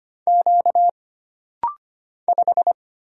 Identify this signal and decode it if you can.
Morse code, heard as a keyed tone, about 25 wpm, 700 Hz. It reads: Q 5
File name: Single Character (Letter:Number:Punctuation:Prosign) - Rapid-Fire - 25wpm